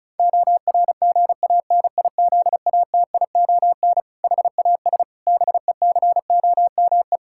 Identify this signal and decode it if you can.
Morse code, heard as a keyed tone, about 36 wpm, 700 Hz. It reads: ORGANIZATION HAS BECOME
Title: Sentences from Top 1000 Words - Review Part 5 - 35wpm